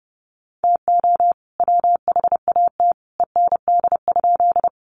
Morse code, heard as a keyed tone, about 30 wpm, 700 Hz. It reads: TO WHAT END?
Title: Sentences from Top 200 Words 30wpm